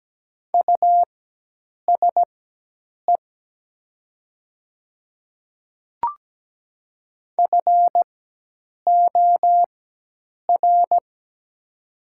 Morse code, heard as a keyed tone, about 17 wpm, spaced as for 8 wpm, 700 Hz. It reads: USE FOR
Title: 4x Character Spacing - Top 100 Words - 17wpm